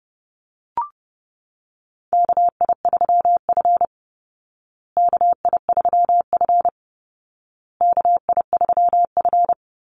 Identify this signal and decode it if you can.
Morse code, heard as a keyed tone, about 28 wpm, 700 Hz. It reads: KI3F KI3F KI3F
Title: Call Signs Repeated 2x - 30wpm